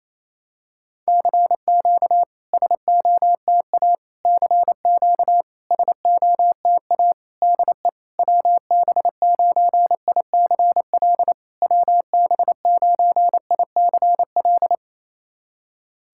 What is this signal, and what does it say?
Morse code, heard as a keyed tone, about 28 wpm, 660 Hz. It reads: CQ SOTA CQ SOTA DE WB9ICL WB9ICL
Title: SOTA QSO Practice 28wpm